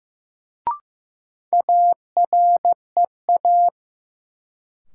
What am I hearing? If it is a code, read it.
Morse code, heard as a keyed tone, about 15 wpm, 700 Hz. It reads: AREA
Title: Sentences from Top 1000 Words - Review Part 2 - New Word Drill - 15wpm